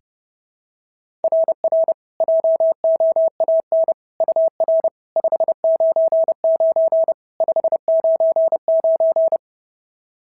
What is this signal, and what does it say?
Morse code, heard as a keyed tone, about 30 wpm, 650 Hz. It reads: RR JOAN UR 599 599
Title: SOTA QSO Practice 30wpm